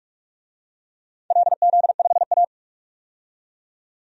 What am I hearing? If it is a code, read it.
Morse code, heard as a keyed tone, about 44 wpm, 700 Hz. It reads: LZ5A
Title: DXCC-100 Call Signs - 4 Characters - 45wpm